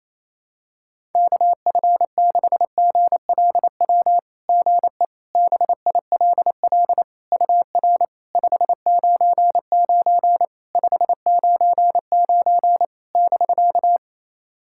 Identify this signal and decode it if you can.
Morse code, heard as a keyed tone, about 28 wpm, 660 Hz. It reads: KF6GLW GE BILL UR 599 599 <BK>
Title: SOTA QSO Practice 28wpm